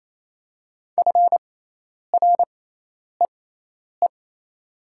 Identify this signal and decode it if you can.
Morse code, heard as a keyed tone, about 28 wpm, spaced as for 10 wpm, 700 Hz. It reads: FREE
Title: 6x Character Spacing - Top 500 Words - 28wpm